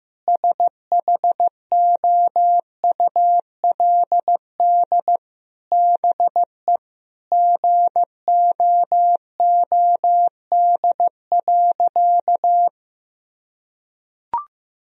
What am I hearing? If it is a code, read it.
Morse code, heard as a keyed tone, about 15 wpm, 700 Hz. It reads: SHOULD BE GOOD.
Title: Sentences from Top 1000 Words - Review Part 1 - 15wpm